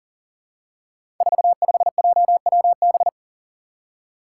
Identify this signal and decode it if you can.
Morse code, heard as a keyed tone, about 40 wpm, 690 Hz. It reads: 45JWB
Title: Five Challenging Letter-Number Combinations 40wpm